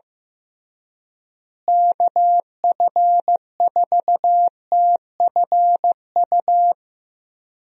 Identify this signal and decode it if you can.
Morse code, heard as a keyed tone, about 15 wpm, 700 Hz. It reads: KF4TFU
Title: USA Call Signs - 6 Characters - Repeated 3x - 15wpm